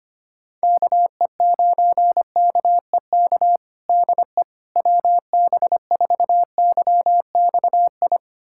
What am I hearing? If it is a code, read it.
Morse code, heard as a keyed tone, about 25 wpm, 700 Hz. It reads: KE9KEK DE WB4YXI
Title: USA Callsign DE Callsign - 6 Characters - 25wpm